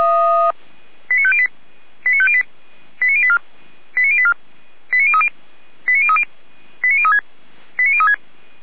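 Five-tone selective call (ZVEI), heard as quick runs of five tones, tones of about 80 ms. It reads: 89508 89508 89084 89084 89030 89030 89036 89036
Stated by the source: ZVEI five-tone selective call received by SDR, from audio.f85360.feuerwehr-orlamuende.org fire brigade server.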